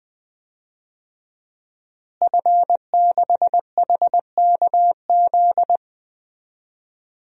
Morse code, heard as a keyed tone, about 21 wpm, 700 Hz. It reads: F6HKZ
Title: DXCC-100 Call Signs - 5 Characters - Repeated 2x - 20wpm